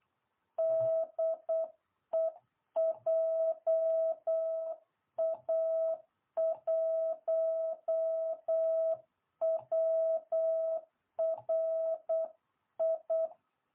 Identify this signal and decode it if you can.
Morse code, heard as a keyed tone, about 8 wpm, 660 Hz. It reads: DE JA1WRI